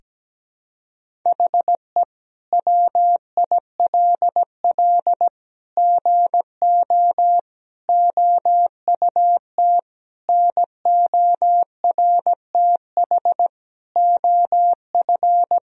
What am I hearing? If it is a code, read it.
Morse code, heard as a keyed tone, about 17 wpm, 700 Hz. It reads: HE WILL GO OUT NORTH OF
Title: Sentences from Top 1000 Words - Review Part 1 - 17wpm